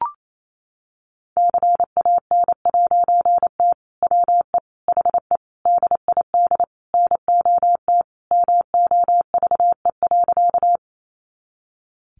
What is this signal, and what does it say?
Morse code, heard as a keyed tone, about 28 wpm, 700 Hz. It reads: CAN'T WE HE DID NOT MOVE.
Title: A Sound Of Thunder 28wpm